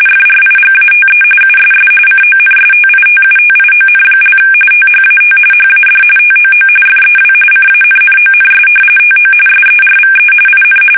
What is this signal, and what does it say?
RTTY, about 45.45 baud. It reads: FRIGATE USS CONSTITUTION) THAT HAD TOO FEW CANNONS TO SERVE IN
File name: fsk850 / battleshipnj